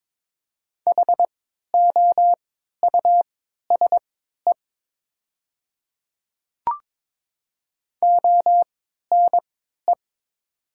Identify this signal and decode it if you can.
Morse code, heard as a keyed tone, about 22 wpm, spaced as for 12 wpm, 700 Hz. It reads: HOUSE ONE
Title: 3x Character Spacing - Top 200 Words - 22wpm